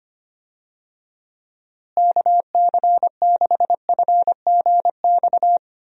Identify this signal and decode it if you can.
Morse code, heard as a keyed tone, about 25 wpm, 700 Hz. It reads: KC6FGX
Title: USA Call Signs - 6 Characters - Repeated 3x - 25wpm